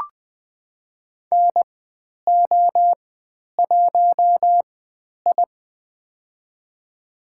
Morse code, heard as a keyed tone, about 20 wpm, spaced as for 10 wpm, 700 Hz. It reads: NO1I
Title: 102 Lesson 17g - Letter u - Call Signs 20wpm